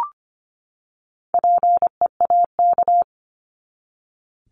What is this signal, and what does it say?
Morse code, heard as a keyed tone, about 25 wpm, 700 Hz. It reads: PEAK
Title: Top 1751-2000 Words 25wpm